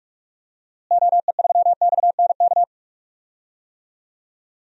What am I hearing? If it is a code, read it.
Morse code, heard as a keyed tone, about 46 wpm, 700 Hz. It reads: OE3XNK
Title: DXCC-100 Call Signs - 6 Characters - Repeated 3x - 45wpm